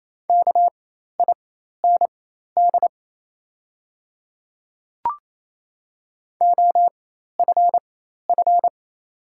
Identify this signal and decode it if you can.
Morse code, heard as a keyed tone, about 28 wpm, spaced as for 13 wpm, 690 Hz. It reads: KIND OFF
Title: 4x Character Spacing - Top 300 Words - 28wpm